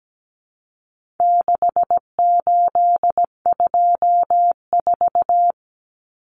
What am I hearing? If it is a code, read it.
Morse code, heard as a keyed tone, about 17 wpm, 700 Hz. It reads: 6824
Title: Common Cut Numbers - 4-Digit 17wpm